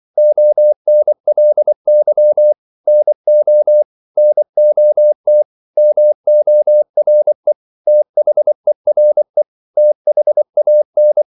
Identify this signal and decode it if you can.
Morse code, heard as a keyed tone, about 24 wpm, 600 Hz. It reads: ONLY NO NOT MORE THERE THAN
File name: CW / Practice / 500MostCommonEnglishWords24